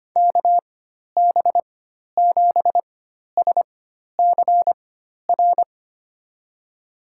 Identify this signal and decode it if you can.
Morse code, heard as a keyed tone, about 25 wpm, spaced as for 12 wpm, 700 Hz. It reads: KB7SCR
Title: USA Call Signs - 6 Characters - 4x Character Spacing - 25wpm